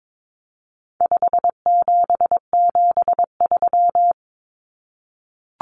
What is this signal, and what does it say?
Morse code, heard as a keyed tone, about 22 wpm, 700 Hz. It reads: 5773
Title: Common Cut Numbers - 4-Digit 22wpm